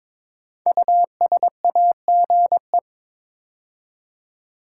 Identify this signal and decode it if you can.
Morse code, heard as a keyed tone, about 22 wpm, 700 Hz. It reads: USAGE
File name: 5 Letter Words - First 3 Letters Given - 22wpm